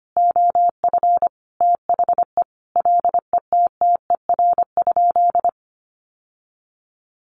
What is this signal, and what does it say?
Morse code, heard as a keyed tone, about 25 wpm, 700 Hz. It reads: OF THE LETTER?
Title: Sentences from Top 900 Words 25wpm